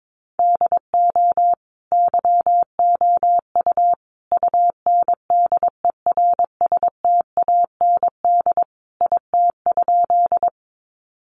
Morse code, heard as a keyed tone, about 22 wpm, 700 Hz. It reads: DO YOU UNDERSTAND IT?
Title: Sentences from Top 600 Words 22wpm